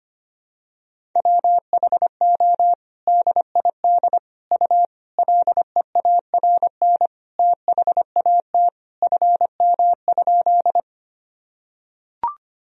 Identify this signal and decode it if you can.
Morse code, heard as a keyed tone, about 25 wpm, 700 Hz. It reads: WHO DID U LEARN THAT FM?
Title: Sentences from Top 300 Words and Abbreviations 25wpm